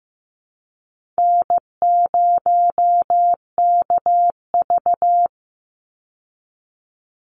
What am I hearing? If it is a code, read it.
Morse code, heard as a keyed tone, about 15 wpm, 700 Hz. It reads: N0KV
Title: Easy Call Signs - 15wpm